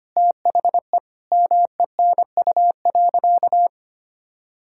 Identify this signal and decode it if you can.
Morse code, heard as a keyed tone, about 25 wpm, 700 Hz. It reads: THE MENU.
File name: practice / 25wpm / Sentences with 4-Word Phrases - 25wpm